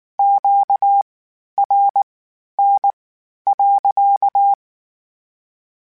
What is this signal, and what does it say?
Morse code, heard as a keyed tone, about 19 wpm, spaced as for 11 wpm, 800 Hz. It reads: QRN.
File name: Q Calls / Q Calls-mp3s / Q Calls sppc @18